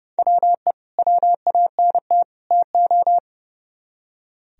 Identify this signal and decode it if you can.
Morse code, heard as a keyed tone, about 38 wpm, 700 Hz. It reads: WE WANT TO